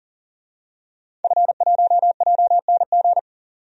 Morse code, heard as a keyed tone, about 38 wpm, 700 Hz. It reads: F1JNG
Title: DXCC-100 Call Signs - 5 Characters - 40wpm